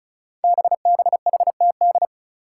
Morse code, heard as a keyed tone, about 36 wpm, 700 Hz. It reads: BBHTD
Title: Five Challenging Letter-Number Combinations 35wpm